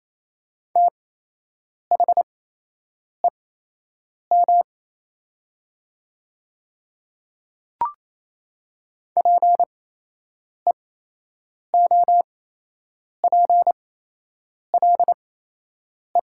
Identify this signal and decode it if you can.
Morse code, heard as a keyed tone, about 28 wpm, spaced as for 8 wpm, 700 Hz. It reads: THEM PEOPLE
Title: 8x Character Spacing - Top 300 Words - 28wpm